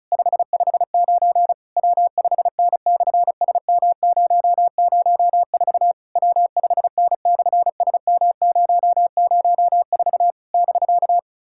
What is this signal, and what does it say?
Morse code, heard as a keyed tone, about 36 wpm, 700 Hz. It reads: 559 W5N/SM004 W5N/SM004 <BK>
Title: SOTA QSO Practice 35wpm